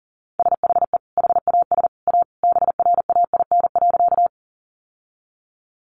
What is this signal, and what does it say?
Morse code, heard as a keyed tone, about 40 wpm, 700 Hz. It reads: SHE HAS A BRAIN.
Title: Sentences from Top 1100 Words 40wpm